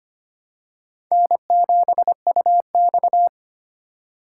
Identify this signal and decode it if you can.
Morse code, heard as a keyed tone, about 25 wpm, 700 Hz. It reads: N7UX
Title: USA Call Signs - 4 Characters - 3x Character Spacing - 25wpm